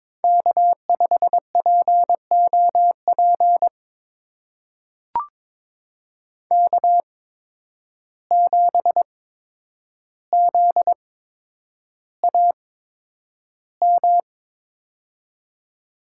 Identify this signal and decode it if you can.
Morse code, heard as a keyed tone, about 22 wpm, spaced as for 9 wpm, 700 Hz. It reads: K5POP K7ZAM
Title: USA Call Signs - 5 Characters - 8x Character Spacing - 22wpm